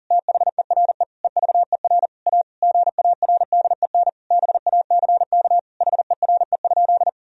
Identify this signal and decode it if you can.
Morse code, heard as a keyed tone, about 40 wpm, 700 Hz. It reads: THERE EVER A GARDEN BACK HERE?
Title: Sentences from Top 1000 Words - Review Part 4 - 40wpm